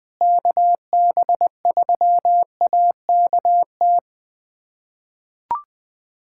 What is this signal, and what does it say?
Morse code, heard as a keyed tone, about 21 wpm, 700 Hz. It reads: KB3AKT
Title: USA Callsign DE Callsign 20wpm